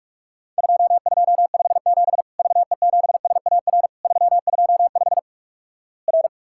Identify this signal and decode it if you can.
Morse code, heard as a keyed tone, about 46 wpm, 700 Hz. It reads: 2257 VE7SAR 325 R
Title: Worked All Europe DX Contest - QTC Exchange - 45wpm